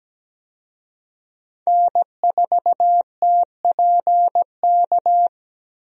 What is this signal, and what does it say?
Morse code, heard as a keyed tone, about 17 wpm, 700 Hz. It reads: N4TPK
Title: USA Call Signs - 5 Characters - 17wpm